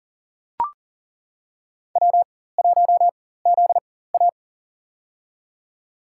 Morse code, heard as a keyed tone, about 37 wpm, spaced as for 18 wpm, 700 Hz. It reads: W1ZA
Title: USA Call Signs - 4 Characters - 4x Character Spacing - 40wpm